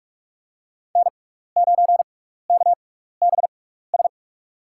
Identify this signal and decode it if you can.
Morse code, heard as a keyed tone, about 46 wpm, spaced as for 15 wpm, 700 Hz. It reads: N9KBS
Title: USA Call Signs - 5 Characters - 6x Character Spacing - 45wpm